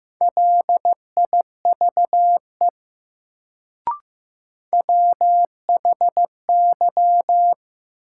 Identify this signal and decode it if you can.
Morse code, heard as a keyed tone, about 15 wpm, 700 Hz. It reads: LIVE WHY
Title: Sentences from Top 200 Words - New Word Drill - 15wpm